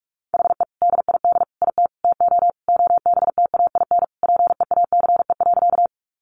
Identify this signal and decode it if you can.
Morse code, heard as a keyed tone, about 44 wpm, 700 Hz. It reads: HE DID IT TO OBTAIN PEACE.
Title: Sentences from Top 1500 Words 45wpm